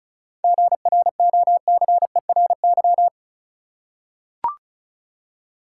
Morse code, heard as a keyed tone, about 36 wpm, 700 Hz. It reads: GROCERY
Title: Seven Letter Words - 35wpm